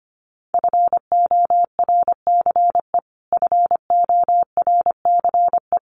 Morse code, heard as a keyed tone, about 25 wpm, 700 Hz. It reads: FORCE FORCE
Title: Top 500 Words Repeated 25wpm